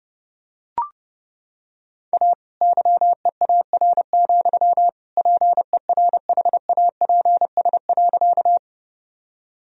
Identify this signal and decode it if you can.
Morse code, heard as a keyed tone, about 30 wpm, 700 Hz. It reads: A YEAR, PERHAPS.
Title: Sentences from Top 1000 Words - Review Part 1 - 30wpm